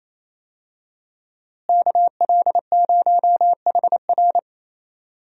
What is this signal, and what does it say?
Morse code, heard as a keyed tone, about 28 wpm, 700 Hz. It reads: KL0HR
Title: USA Call Signs - 5 Characters - Repeated 3x - 28wpm